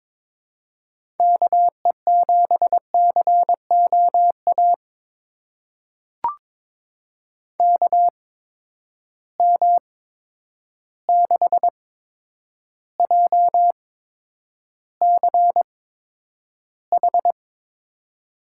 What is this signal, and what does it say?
Morse code, heard as a keyed tone, about 22 wpm, spaced as for 9 wpm, 700 Hz. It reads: KE7COA KM6JCH
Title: USA Call Signs - 6 Characters - 8x Character Spacing - 22wpm